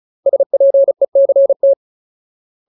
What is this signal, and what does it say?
Morse code, heard as a keyed tone, about 34 wpm, 550 Hz. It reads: SPECT
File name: Ultimate Word List / 35WPM Ultimate